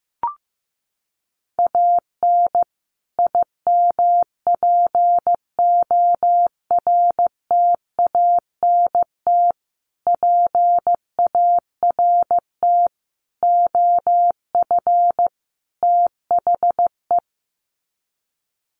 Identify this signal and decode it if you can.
Morse code, heard as a keyed tone, about 15 wpm, 700 Hz. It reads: AN IMPORTANT PART OF THE
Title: Sets of 5 Words Encore 15wpm